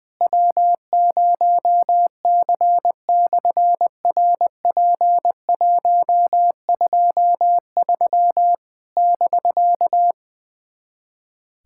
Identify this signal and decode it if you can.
Morse code, heard as a keyed tone, about 20 wpm, 700 Hz. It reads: W0C/RP123 <BK>
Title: SOTA QSO Practice 20wpm